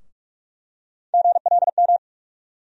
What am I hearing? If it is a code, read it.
Morse code, heard as a keyed tone, about 42 wpm, 700 Hz. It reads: GLM
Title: USA Call Sign Postfixes - 3 Letters 45wpm